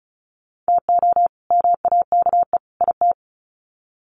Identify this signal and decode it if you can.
Morse code, heard as a keyed tone, about 34 wpm, 700 Hz. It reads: TO MAKE IT
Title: Sets of 3 Words 35wpm